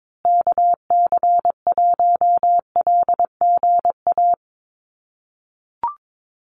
Morse code, heard as a keyed tone, about 22 wpm, 700 Hz. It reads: KC1LGA
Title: USA Call Signs Repeated 3x 22wpm